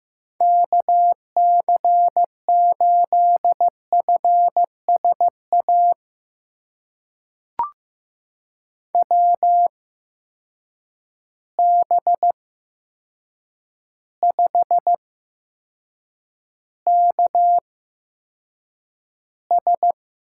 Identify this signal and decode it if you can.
Morse code, heard as a keyed tone, about 15 wpm, spaced as for 7 wpm, 700 Hz. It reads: KC8FSA WB5KS
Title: USA Call Signs - 8x Character Spacing - 15wpm